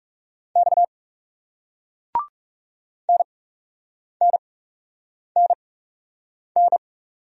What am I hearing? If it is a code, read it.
Morse code, heard as a keyed tone, about 38 wpm, spaced as for 8 wpm, 700 Hz. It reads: X NNNN